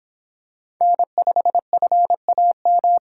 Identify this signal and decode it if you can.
Morse code, heard as a keyed tone, about 26 wpm, 700 Hz. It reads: N5FAM